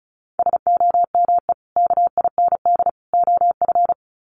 Morse code, heard as a keyed tone, about 34 wpm, 700 Hz. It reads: SOME KIND OF
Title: Sets of 3 Words 35wpm